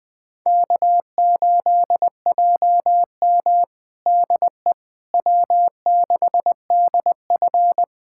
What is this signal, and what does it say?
Morse code, heard as a keyed tone, about 20 wpm, 700 Hz. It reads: K8JM DE W6DF
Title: USA Callsign DE Callsign - 4 Characters - 20wpm